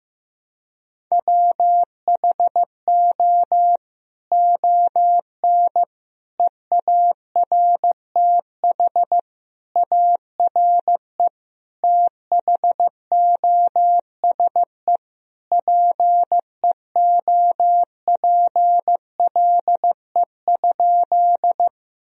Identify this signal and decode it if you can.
Morse code, heard as a keyed tone, about 15 wpm, 700 Hz. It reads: WHO ON EARTH ARE THOSE PEOPLE?
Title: Sentences from Top 1000 Words - Review Part 1 - 15wpm